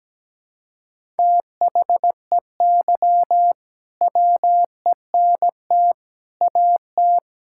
Morse code, heard as a keyed tone, about 17 wpm, 700 Hz. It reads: THEY WENT AT
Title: Sentences from Top 1000 Words - Review Part 5 - 17wpm